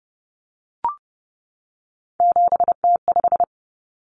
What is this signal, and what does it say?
Morse code, heard as a keyed tone, about 30 wpm, 700 Hz. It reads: 7T5
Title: Common Cut Numbers - 3-Digit 30wpm